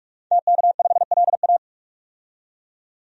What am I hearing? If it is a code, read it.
Morse code, heard as a keyed tone, about 42 wpm, 700 Hz. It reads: TK5LA
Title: DXCC-100 Call Signs - 5 Characters - 45wpm